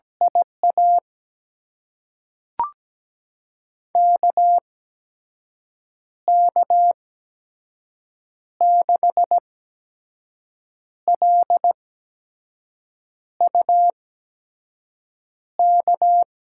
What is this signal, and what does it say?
Morse code, heard as a keyed tone, about 17 wpm, spaced as for 5 wpm, 700 Hz. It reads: IA KK6LUK
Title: USA Call Signs - 6 Characters - 8x Character Spacing - 17wpm